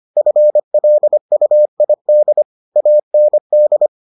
Morse code, heard as a keyed tone, about 25 wpm, 600 Hz. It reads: FLUID AND